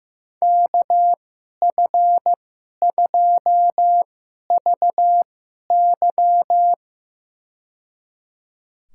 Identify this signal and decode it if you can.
Morse code, heard as a keyed tone, about 15 wpm, spaced as for 11 wpm, 700 Hz. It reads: KF2VY
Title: USA Call Signs - 5 Characters - 2x Character Spacing - 15wpm